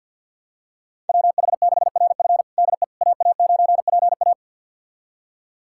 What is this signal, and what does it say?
Morse code, heard as a keyed tone, about 48 wpm, 700 Hz. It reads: WH6RF DE AA9PA
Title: USA Callsign DE Callsign - 5 Characters - 50wpm